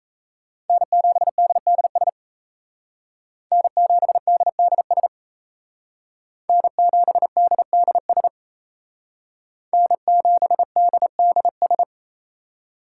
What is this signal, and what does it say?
Morse code, heard as a keyed tone, about 33 wpm, 700 Hz. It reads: N7DDS N7DDS N7DDS N7DDS
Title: Speed Racing - USA Call Signs 28wpm